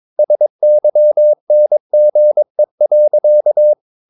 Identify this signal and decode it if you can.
Morse code, heard as a keyed tone, about 22 wpm, 600 Hz. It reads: SYNGE.
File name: Ferie QRS 22wpm